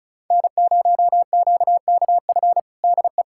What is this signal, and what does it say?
Morse code, heard as a keyed tone, about 36 wpm, 700 Hz. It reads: N0QKF DE
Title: USA Callsign DE Callsign - 5 Characters - 35wpm